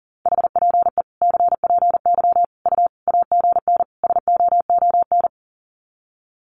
Morse code, heard as a keyed tone, about 40 wpm, 700 Hz. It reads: HPE CPY U AGN SOON
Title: Common QSO Phrases 40wpm